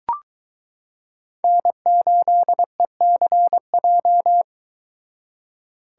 Morse code, heard as a keyed tone, about 22 wpm, 700 Hz. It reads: N8ECJ